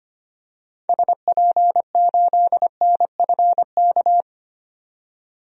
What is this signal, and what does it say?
Morse code, heard as a keyed tone, about 25 wpm, 700 Hz. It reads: SP8NFK
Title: DXCC-100 Call Signs - 6 Characters - 25wpm